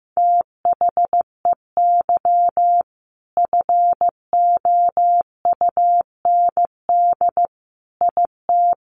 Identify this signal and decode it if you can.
Morse code, heard as a keyed tone, about 15 wpm, 700 Hz. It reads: THEY FOUND IT
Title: Sentences from Top 1400 Words 15wpm